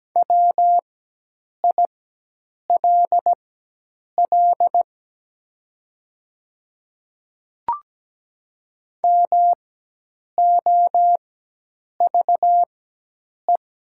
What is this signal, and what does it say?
Morse code, heard as a keyed tone, about 17 wpm, spaced as for 8 wpm, 700 Hz. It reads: WILL MOVE
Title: 4x Character Spacing - Top 300 Words - 17wpm